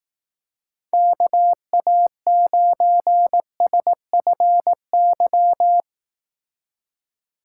Morse code, heard as a keyed tone, about 18 wpm, 700 Hz. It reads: KA9SFY
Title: Speed Racing - USA 2x3 Call Signs 15wpm